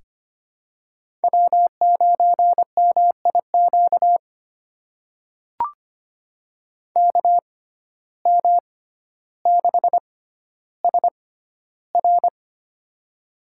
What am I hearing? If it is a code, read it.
Morse code, heard as a keyed tone, about 25 wpm, spaced as for 13 wpm, 700 Hz. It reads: W9MIQ KM6SR
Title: USA Call Signs - 5 Characters - 6x Character Spacing - 25wpm